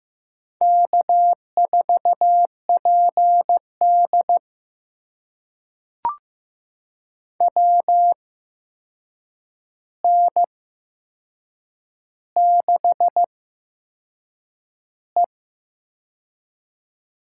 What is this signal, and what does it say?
Morse code, heard as a keyed tone, about 15 wpm, spaced as for 6 wpm, 700 Hz. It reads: K4PD WN6E
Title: USA Call Signs - 4 Characters - 8x Character Spacing - 15wpm